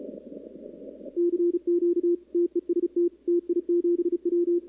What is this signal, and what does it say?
Morse code, heard as a keyed tone, about 30 wpm, 340 Hz, with radio noise. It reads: CQ TEST TI7W
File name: contests / 2019-cw-wpx-kt5j-k5pi- -2330 0-11